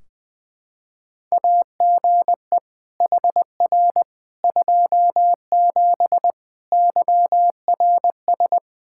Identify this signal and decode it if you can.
Morse code, heard as a keyed tone, about 20 wpm, 700 Hz. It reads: AGE HR 27 YRS
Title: Practice QSO - Exchange 3 - More Info - 20wpm